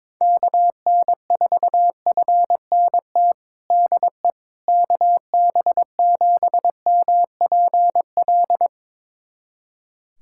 Morse code, heard as a keyed tone, about 21 wpm, 700 Hz. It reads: KN4FNT DE KB7MPL
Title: USA Callsign DE Callsign - 6 Characters - 22wpm